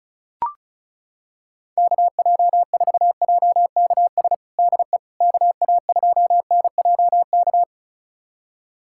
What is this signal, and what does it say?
Morse code, heard as a keyed tone, about 36 wpm, 700 Hz. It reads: KJ4JKS DE KA2NJK
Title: USA Callsign DE Callsign - 6 Characters - 35wpm